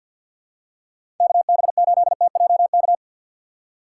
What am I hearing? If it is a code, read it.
Morse code, heard as a keyed tone, about 47 wpm, 700 Hz. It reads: KB8TJK
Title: USA Call Signs - 6 Characters - 6x Character Spacing - 50wpm